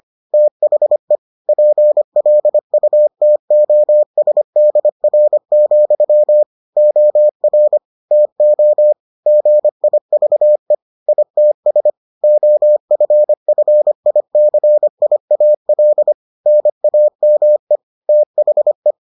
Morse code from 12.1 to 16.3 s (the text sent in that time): OFFICIAL